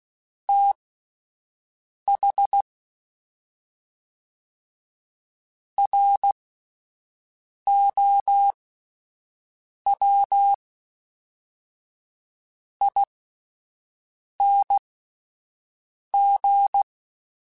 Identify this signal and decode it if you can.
Morse code, heard as a keyed tone, about 16 wpm, spaced as for 4 wpm, 800 Hz. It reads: THROWING